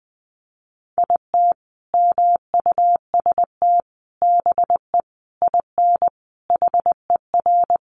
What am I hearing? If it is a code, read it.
Morse code, heard as a keyed tone, about 20 wpm, 700 Hz. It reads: IT MUST BE IN HER
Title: Sentences from Top 300 Words 20wpm